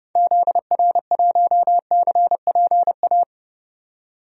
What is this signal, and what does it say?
Morse code, heard as a keyed tone, about 30 wpm, 700 Hz. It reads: ZR1CPA
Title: DXCC-100 Call Signs - 6 Characters - 30wpm